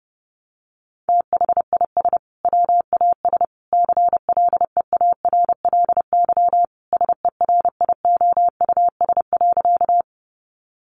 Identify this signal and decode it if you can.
Morse code, heard as a keyed tone, about 30 wpm, 700 Hz. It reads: THIS WAS CLEARLY SERIOUS.
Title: Sentences from Top 1300 Words 30wpm